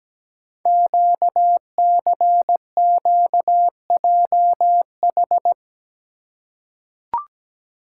Morse code, heard as a keyed tone, about 17 wpm, 700 Hz. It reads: QCQJH
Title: Code Groups - 5 Letters - 17wpm